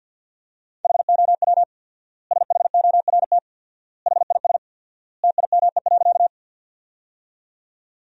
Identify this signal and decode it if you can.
Morse code, heard as a keyed tone, about 50 wpm, 700 Hz. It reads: HOW SHORT HIS TIME.
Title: 4x Word-Spacing - Sentences from Top 400 Words - 50wpm